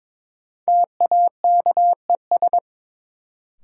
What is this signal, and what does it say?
Morse code, heard as a keyed tone, about 22 wpm, 700 Hz. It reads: TAKES
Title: Sentences from Top 1000 Words - Review Part 6 - New Word Drill - 22wpm